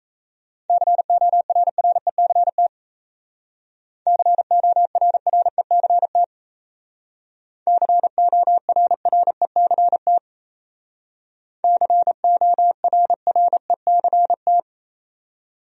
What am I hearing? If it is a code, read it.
Morse code, heard as a keyed tone, about 37 wpm, 700 Hz. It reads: CORRECT CORRECT CORRECT CORRECT